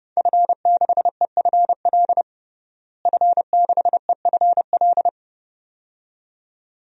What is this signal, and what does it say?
Morse code, heard as a keyed tone, about 31 wpm, 700 Hz. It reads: F6EFL F6EFL
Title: DXCC-100 Call Signs - 5 Characters - Repeated 2x - 30wpm